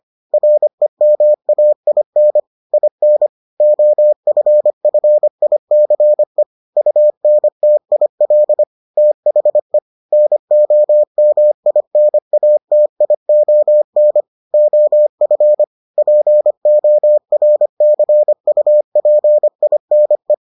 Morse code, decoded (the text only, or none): REMAIN IN OFFICE UNTIL THE NOMINATION OF PORCUPINE